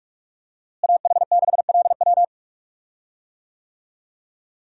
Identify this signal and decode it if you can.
Morse code, heard as a keyed tone, about 46 wpm, 700 Hz. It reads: AH6LW
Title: USA Call Signs - 5 Characters - Repeated 2x - 45wpm